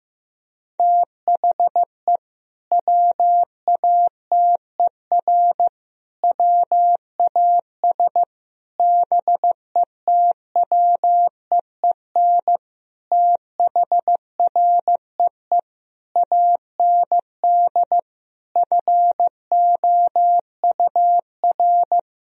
Morse code, heard as a keyed tone, about 15 wpm, 700 Hz. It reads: THE WATER WAS BETWEEN THREE AND FOUR
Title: Sentences from Top 1000 Words - Review Part 3 - 15wpm